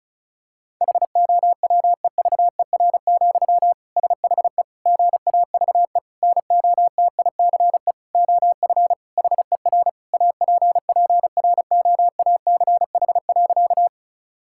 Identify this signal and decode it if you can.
Morse code, heard as a keyed tone, about 36 wpm, 700 Hz. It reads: HOWEVER, SHE GAVE NOTICE OF HER APPROACH.
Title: Sentences from Top 1000 Words - Review Part 6 - 35wpm